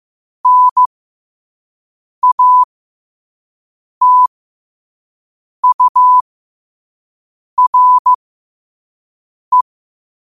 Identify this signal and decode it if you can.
Morse code, heard as a keyed tone, about 15 wpm, spaced as for 5 wpm, 1000 Hz. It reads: NATURE
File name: qotdfiles / 5wpm